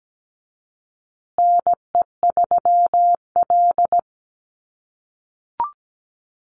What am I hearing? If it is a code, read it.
Morse code, heard as a keyed tone, about 17 wpm, 700 Hz. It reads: NE3L
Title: USA Call Signs Repeated 3x 17wpm